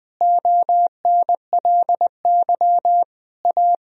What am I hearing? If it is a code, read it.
Morse code, heard as a keyed tone, about 20 wpm, 700 Hz. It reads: ONLY A
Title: Sentences from Top 1000 Words - Review Part 2 - 20wpm